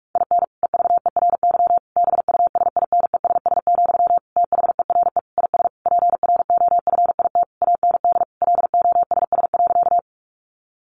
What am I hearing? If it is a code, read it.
Morse code, heard as a keyed tone, about 44 wpm, 700 Hz. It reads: IN EVERY BUSINESS, THERE IS PROFIT AND LOSS.
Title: Sentences from Top 1400 Words 45wpm